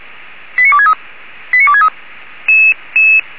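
Five-tone selective call (ZVEI), heard as quick runs of five tones, tones of about 70 ms. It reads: 89263 89263
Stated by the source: ZVEI five-tone selective call received by SDR.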